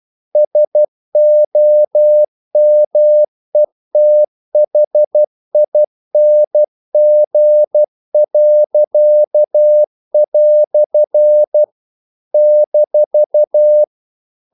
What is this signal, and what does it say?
Morse code, heard as a keyed tone, about 12 wpm, 600 Hz. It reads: SOMETHING." -
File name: CW / Practice / EnglishQuotations0112